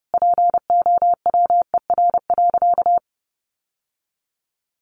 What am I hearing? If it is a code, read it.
Morse code, heard as a keyed tone, about 28 wpm, 700 Hz. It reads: POWER.
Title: Sentences from Top 600 Words 30wpm